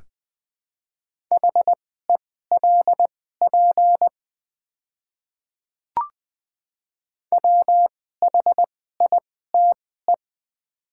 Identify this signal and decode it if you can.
Morse code, heard as a keyed tone, about 20 wpm, spaced as for 15 wpm, 690 Hz. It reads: HELP WHITE
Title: 2x Character Spacing - Top 500 Words - 20wpm